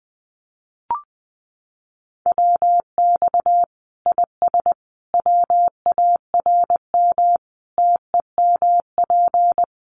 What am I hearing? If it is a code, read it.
Morse code, heard as a keyed tone, about 20 wpm, 700 Hz. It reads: WX IS WARM TEMP
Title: Common QSO Phrases 20wpm